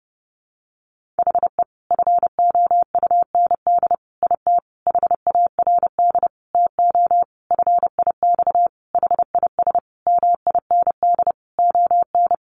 Morse code, heard as a keyed tone, about 30 wpm, 700 Hz. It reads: HE FOUND IT HARD TO FIX HIS MIND ON
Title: Sentences from Top 1400 Words 30wpm